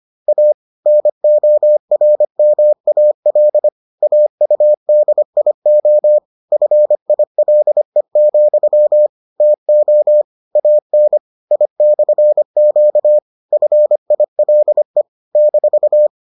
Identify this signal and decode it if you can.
Morse code, heard as a keyed tone, about 25 wpm, 600 Hz. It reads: A NORMAL AUDIO FILE, TO AN I/Q FILE -